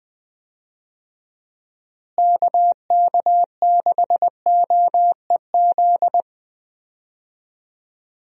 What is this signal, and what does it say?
Morse code, heard as a keyed tone, about 20 wpm, 700 Hz. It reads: KK6OEZ
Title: USA Call Signs - 6 Characters - Repeated 3x - 20wpm